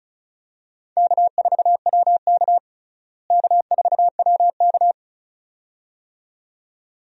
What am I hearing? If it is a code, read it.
Morse code, heard as a keyed tone, about 36 wpm, 700 Hz. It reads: K4WK K4WK
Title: USA Call Signs - 4 Characters - Repeated 2x - 35wpm